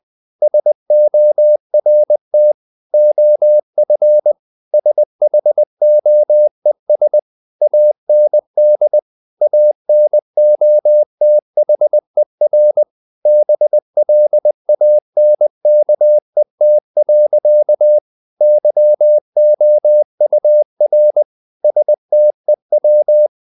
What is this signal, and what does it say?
Morse code, heard as a keyed tone, about 20 wpm, 600 Hz. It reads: SORT OF SHOES AND ANOTHER BLANKET. YOUR STEW
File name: the old man and the sea / the old man and the sea part 1 20wpm